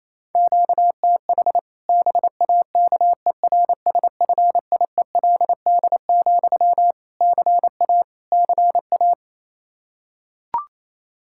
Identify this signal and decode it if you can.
Morse code, heard as a keyed tone, about 28 wpm, 700 Hz. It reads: QTH BAKERSFIELD, CA CA
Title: Practice QSO - Exchange 1 - Beginning - 28wpm